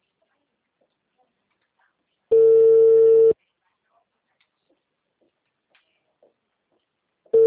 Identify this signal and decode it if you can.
Ringback tone (China, 450 Hz), with speech.
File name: out- -2009- - - .56228